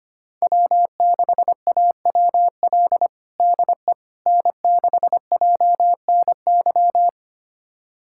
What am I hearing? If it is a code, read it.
Morse code, heard as a keyed tone, about 25 wpm, 700 Hz. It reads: W6AWL DE N6JNY
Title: USA Callsign DE Callsign - 5 Characters - 25wpm